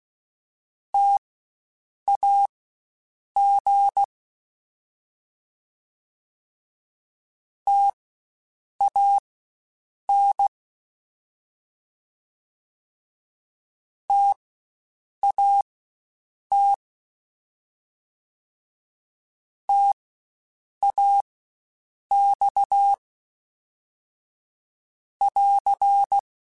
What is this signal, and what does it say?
Morse code, heard as a keyed tone, about 16 wpm, spaced as for 7 wpm, 800 Hz. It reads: TAG TAN TAT TAX <AR>